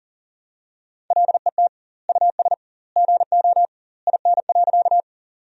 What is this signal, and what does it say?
Morse code, heard as a keyed tone, about 37 wpm, 700 Hz. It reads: LET US GO IN.
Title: 2x Word-Spacing - Sentences from Top 300 Words - 40wpm